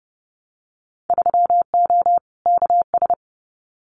Morse code, heard as a keyed tone, about 30 wpm, 700 Hz. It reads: 3O KS
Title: Winter Field Day Exchanges Encore 30wpm